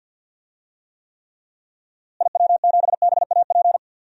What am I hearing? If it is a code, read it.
Morse code, heard as a keyed tone, about 50 wpm, 700 Hz. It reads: IW7BAP
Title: DXCC-100 Call Signs - 6 Characters - Repeated 3x - 50wpm